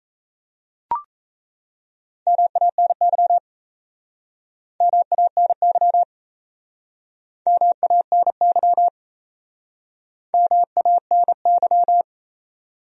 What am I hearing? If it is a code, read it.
Morse code, heard as a keyed tone, about 39 wpm, 700 Hz. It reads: MANY MANY MANY MANY